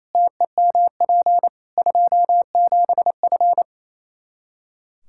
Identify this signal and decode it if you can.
Morse code, heard as a keyed tone, about 28 wpm, 700 Hz. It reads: TEMP 27F
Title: Practice QSO - Exchange 2 - Info - 28wpm